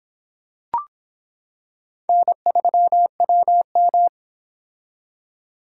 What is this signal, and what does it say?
Morse code, heard as a keyed tone, about 25 wpm, 700 Hz. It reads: N3WM